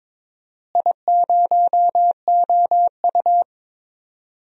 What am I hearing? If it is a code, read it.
Morse code, heard as a keyed tone, about 22 wpm, 700 Hz. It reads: I0OU
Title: DXCC-100 Call Signs - 4 Characters - Repeated 2x - 22wpm